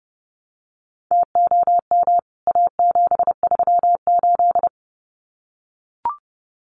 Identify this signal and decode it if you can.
Morse code, heard as a keyed tone, about 29 wpm, 700 Hz. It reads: TOM A738
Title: CWT Exchange 30wpm